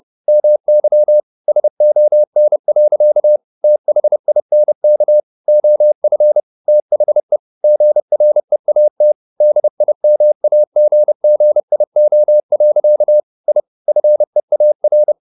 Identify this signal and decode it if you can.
Morse code, heard as a keyed tone, about 30 wpm, 600 Hz. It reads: MY SON. THINK OF THE GREAT DIMAGGIO. I FEAR
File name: the old man and the sea / the old man and the sea 1 30wpm